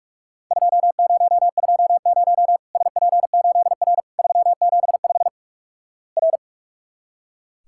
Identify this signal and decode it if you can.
Morse code, heard as a keyed tone, about 46 wpm, 700 Hz. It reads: 2020 SP8R 375 R
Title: Worked All Europe DX Contest - QTC Exchange - 45wpm